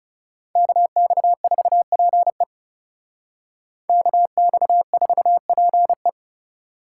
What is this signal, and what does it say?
Morse code, heard as a keyed tone, about 34 wpm, 700 Hz. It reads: KX4PE KX4PE
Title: Speed Racing - USA Call Signs 30wpm